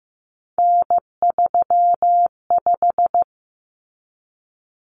Morse code, heard as a keyed tone, about 15 wpm, 700 Hz. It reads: N35
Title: Common Cut Numbers - 3-Digit 15wpm